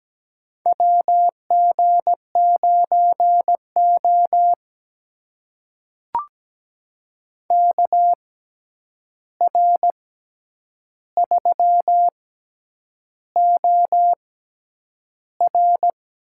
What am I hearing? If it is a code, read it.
Morse code, heard as a keyed tone, about 17 wpm, spaced as for 8 wpm, 700 Hz. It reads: WG9O KR3OR
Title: USA Call Signs - 6x Character Spacing - 17wpm